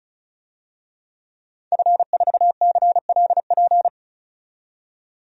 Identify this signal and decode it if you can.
Morse code, heard as a keyed tone, about 34 wpm, 700 Hz. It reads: F4CLP
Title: DXCC-100 Call Signs - 5 Characters - 35wpm